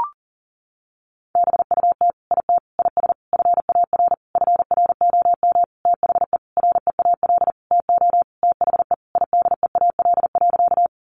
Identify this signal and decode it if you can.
Morse code, heard as a keyed tone, about 40 wpm, 700 Hz. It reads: BUT IT IS FAR FROM THE REAL TO THE IDEAL.
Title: Sentences from Top 1400 Words 40wpm